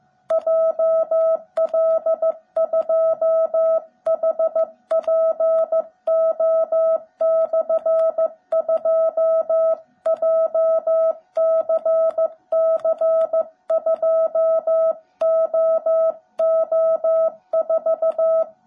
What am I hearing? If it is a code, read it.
Morse code, heard as a keyed tone, about 15 wpm, 650 Hz. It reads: JL2HPO/2 JCC2OO4
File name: jh8cbh / 2021cw / cw17b